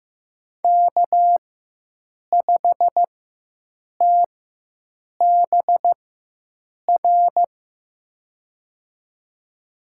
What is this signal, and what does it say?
Morse code, heard as a keyed tone, about 15 wpm, spaced as for 7 wpm, 700 Hz. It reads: K5TBR
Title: USA Call Signs - 5 Characters - 4x Character Spacing - 15wpm